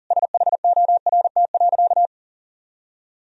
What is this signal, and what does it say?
Morse code, heard as a keyed tone, about 37 wpm, 700 Hz. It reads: SHORT.
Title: Sentences from Top 500 Words and Abbreviations 40wpm